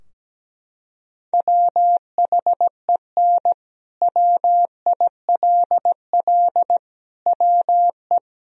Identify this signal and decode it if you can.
Morse code, heard as a keyed tone, about 17 wpm, 700 Hz. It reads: WHEN WILL WE
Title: Sentences from Top 1000 Words - Review Part 3 - 17wpm